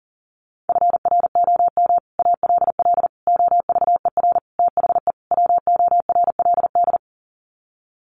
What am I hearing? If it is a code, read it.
Morse code, heard as a keyed tone, about 40 wpm, 700 Hz. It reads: FROM ALL OVER THE WORLD
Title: Sets of 5 Words 40wpm